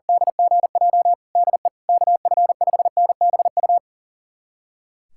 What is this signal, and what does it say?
Morse code, heard as a keyed tone, about 40 wpm, 700 Hz. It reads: DGJ DE KF5NBU
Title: USA Callsign DE Callsign - 6 Characters - 40wpm